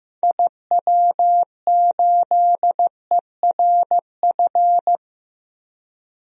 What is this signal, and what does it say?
Morse code, heard as a keyed tone, about 15 wpm, 700 Hz. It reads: IW8ERF
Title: DXCC-100 Call Signs - 6 Characters - Repeated 3x - 15wpm